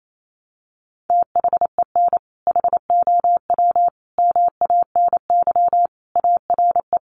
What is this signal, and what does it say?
Morse code, heard as a keyed tone, about 28 wpm, 700 Hz. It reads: THEN HOW MANY ARE
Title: Sentences from Top 100 Words 28wpm